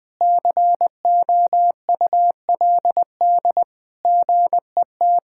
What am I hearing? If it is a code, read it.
Morse code, heard as a keyed tone, about 21 wpm, 700 Hz. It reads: COULD GET
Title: Sentences from Top 1000 Words - Review Part 5 - 20wpm